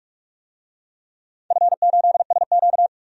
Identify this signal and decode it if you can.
Morse code, heard as a keyed tone, about 43 wpm, 700 Hz. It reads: F8SQ
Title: DXCC-100 Call Signs - 4 Characters - 45wpm